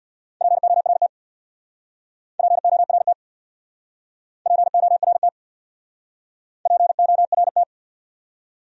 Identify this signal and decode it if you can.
Morse code, heard as a keyed tone, about 66 wpm, 700 Hz. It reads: PORT PORT PORT PORT